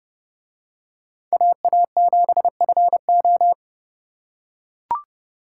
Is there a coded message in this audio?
Yes: Morse code from 1.3 to 3.5 s, heard as a keyed tone, about 28 wpm, 700 Hz. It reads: AA7FO